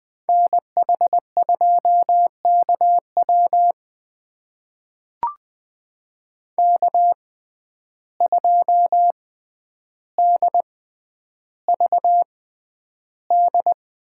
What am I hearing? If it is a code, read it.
Morse code, heard as a keyed tone, about 20 wpm, spaced as for 10 wpm, 700 Hz. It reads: NH2KW K2DVD
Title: USA Call Signs - 5 Characters - 6x Character Spacing - 20wpm